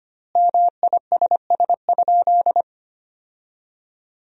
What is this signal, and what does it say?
Morse code, heard as a keyed tone, about 25 wpm, 700 Hz. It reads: MISS?
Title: 4x Word-Spacing - Sentences from Top 500 Words - 25wpm